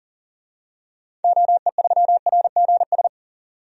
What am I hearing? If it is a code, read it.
Morse code, heard as a keyed tone, about 40 wpm, 700 Hz. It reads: OE3RGS
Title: DXCC-100 Call Signs - 6 Characters - 40wpm